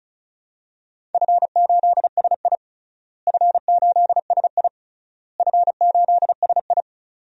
Morse code, heard as a keyed tone, about 34 wpm, 700 Hz. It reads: F8SI F8SI F8SI
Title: DXCC-100 Call Signs - 4 Characters - Repeated 3x - 35wpm